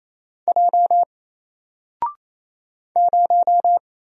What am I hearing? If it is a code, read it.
Morse code, heard as a keyed tone, about 27 wpm, 700 Hz. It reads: J 0
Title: Single Letter-Number Alt 28wpm